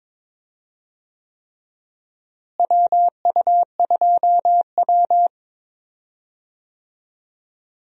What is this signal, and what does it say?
Morse code, heard as a keyed tone, about 22 wpm, 700 Hz. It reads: WU2W
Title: USA Call Signs - 4 Characters - Repeated 3x - 22wpm